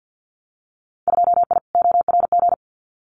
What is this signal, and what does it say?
Morse code, heard as a keyed tone, about 50 wpm, 700 Hz. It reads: 2I ORG
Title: Winter Field Day Exchanges Encore 50wpm